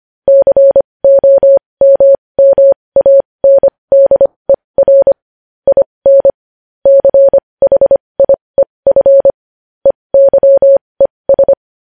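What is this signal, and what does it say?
Morse code, heard as a keyed tone, about 25 wpm, spaced as for 20 wpm, 550 Hz. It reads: COMMANDER IN CHIEF EYES
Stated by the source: Morse code news headlines sent at 20.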